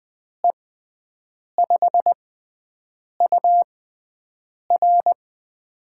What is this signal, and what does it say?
Morse code, heard as a keyed tone, about 20 wpm, spaced as for 7 wpm, 700 Hz. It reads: E5UR
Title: USA Call Signs - 5 Characters - 6x Character Spacing - 20wpm